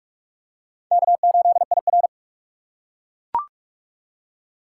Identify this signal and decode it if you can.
Morse code, heard as a keyed tone, about 42 wpm, 700 Hz. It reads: K8IR
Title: USA Call Signs - 4 Characters - 4x Character Spacing - 45wpm